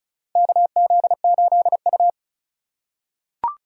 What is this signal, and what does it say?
Morse code, heard as a keyed tone, about 36 wpm, 700 Hz. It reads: KZ8U
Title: USA Call Signs - 4 Characters - 2x Character Spacing - 35wpm